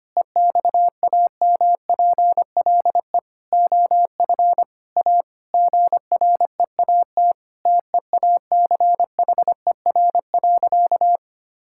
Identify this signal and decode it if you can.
Morse code, heard as a keyed tone, about 25 wpm, 700 Hz. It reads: EXAMPLE OF A GREAT TEACHER.
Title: Sentences from Top 1000 Words - Review Part 6 - 25wpm